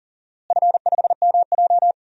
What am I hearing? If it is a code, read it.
Morse code, heard as a keyed tone, about 40 wpm, 700 Hz. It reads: F5MJ
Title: DXCC-100 Call Signs - 4 Characters - Repeated 2x - 40wpm